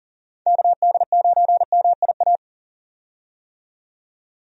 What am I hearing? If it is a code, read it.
Morse code, heard as a keyed tone, about 40 wpm, 700 Hz. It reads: KD9MIA
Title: USA Call Signs - 6 Characters - 40wpm